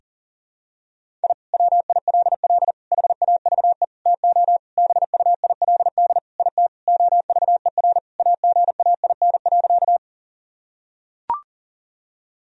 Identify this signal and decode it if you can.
Morse code, heard as a keyed tone, about 37 wpm, 700 Hz. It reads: I WILL HAVE TO BUILD IT OVER AGAIN.
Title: Sentences from Top 1000 Words - Review Part 1 - 40wpm